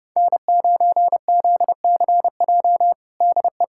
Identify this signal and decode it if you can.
Morse code, heard as a keyed tone, about 30 wpm, 700 Hz. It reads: N9ZCJ DE
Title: USA Callsign DE Callsign - 5 Characters - 30wpm